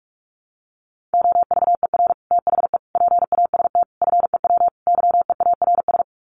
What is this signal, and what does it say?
Morse code, heard as a keyed tone, about 44 wpm, 700 Hz. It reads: OVER THE PAST FEW YEARS
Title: Sets of 5 Words Encore 45wpm